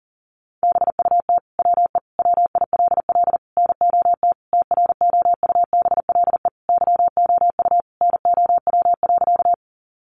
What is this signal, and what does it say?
Morse code, heard as a keyed tone, about 40 wpm, 700 Hz. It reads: BUT WE WILL NOT TROUBLE YOU NOW.
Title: Sentences from Top 1200 Words 40wpm